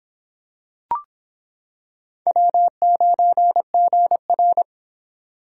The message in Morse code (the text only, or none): W9GR